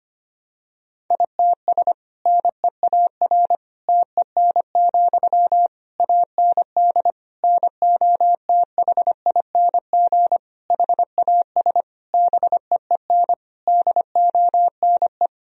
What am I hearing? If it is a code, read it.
Morse code, heard as a keyed tone, about 25 wpm, 700 Hz. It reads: ITS NEAR TEN, AND NOTHING HAS BEEN DONE
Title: Sentences from Top 1000 Words - Review Part 2 - 25wpm